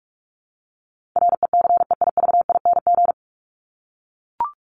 Morse code, heard as a keyed tone, about 44 wpm, 700 Hz. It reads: RECEIVING
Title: ING Suffix - Top 100 Words with Base Words 45wpm